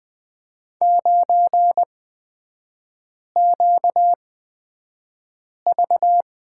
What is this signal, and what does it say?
Morse code, heard as a keyed tone, about 20 wpm, spaced as for 5 wpm, 700 Hz. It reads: 9QV
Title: 255 Lesson 41e - Period - Three Characters 20wpm